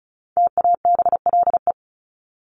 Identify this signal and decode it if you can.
Morse code, heard as a keyed tone, about 34 wpm, 700 Hz. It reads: TABLE
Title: Top 500 Words 35wpm